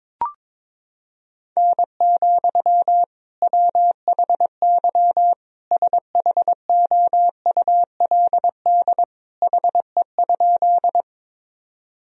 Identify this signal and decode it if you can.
Morse code, heard as a keyed tone, about 22 wpm, 700 Hz. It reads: N, WHY SHOULD HE?
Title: Sentences from Top 300 Words and Abbreviations 22wpm